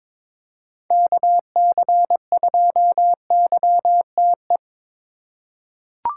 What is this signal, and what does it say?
Morse code, heard as a keyed tone, about 22 wpm, 700 Hz. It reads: KC2YTE
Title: USA Call Signs - 3x Character Spacing - 22wpm